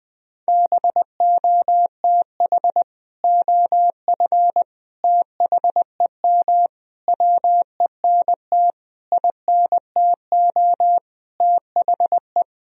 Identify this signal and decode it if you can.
Morse code, heard as a keyed tone, about 21 wpm, 700 Hz. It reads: BOTH OF THEM WENT INTO THE
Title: Sentences from Top 1000 Words - Review Part 1 - 20wpm